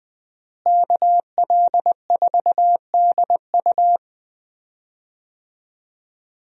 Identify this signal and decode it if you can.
Morse code, heard as a keyed tone, about 20 wpm, 700 Hz. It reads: KL4DU
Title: USA Call Signs - 5 Characters - Repeated 3x - 20wpm